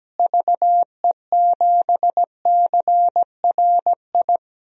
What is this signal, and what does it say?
Morse code, heard as a keyed tone, about 17 wpm, 700 Hz. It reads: VE7CRI
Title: DXCC-100 Call Signs - 6 Characters - Repeated 3x - 17wpm